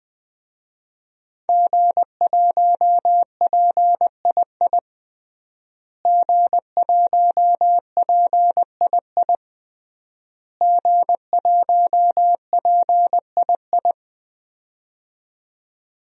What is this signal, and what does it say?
Morse code, heard as a keyed tone, about 20 wpm, 700 Hz. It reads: G1PII G1PII G1PII
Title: DXCC-100 Call Signs - 5 Characters - Repeated 3x - 20wpm